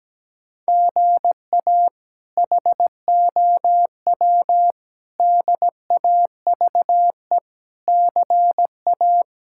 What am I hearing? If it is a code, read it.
Morse code, heard as a keyed tone, about 17 wpm, 700 Hz. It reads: GA HOW DAVE CA
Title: K1USN SST QSO Practice - 17wpm